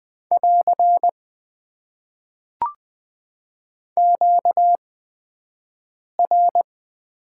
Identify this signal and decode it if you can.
Morse code, heard as a keyed tone, about 20 wpm, spaced as for 5 wpm, 700 Hz. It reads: <AR> QR
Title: Top 90 QSO Elements - 8x Character Spacing - 20wpm